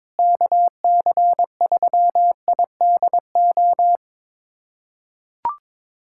Morse code, heard as a keyed tone, about 21 wpm, 700 Hz. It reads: KC3IDO
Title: USA Callsign DE Callsign 22wpm